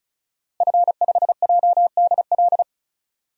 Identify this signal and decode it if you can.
Morse code, heard as a keyed tone, about 36 wpm, 700 Hz. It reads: F5JDL
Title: DXCC-100 Call Signs - 5 Characters - Repeated 2x - 35wpm